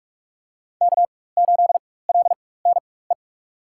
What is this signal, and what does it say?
Morse code, heard as a keyed tone, about 45 wpm, spaced as for 21 wpm, 700 Hz. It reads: K8LNE
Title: USA Call Signs - 5 Characters - 4x Character Spacing - 45wpm